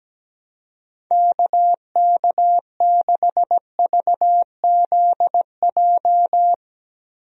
Morse code, heard as a keyed tone, about 17 wpm, 700 Hz. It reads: KK6VZJ
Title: USA Call Signs - 6 Characters - Repeated 2x - 17wpm